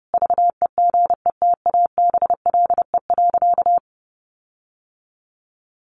Morse code, heard as a keyed tone, about 30 wpm, 700 Hz. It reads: VEGETABLE.
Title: Sentences from Top 1300 Words 30wpm